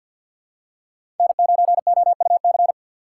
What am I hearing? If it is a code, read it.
Morse code, heard as a keyed tone, about 52 wpm, 700 Hz. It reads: N9OUC
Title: USA Call Signs - 5 Characters - 8x Character Spacing - 50wpm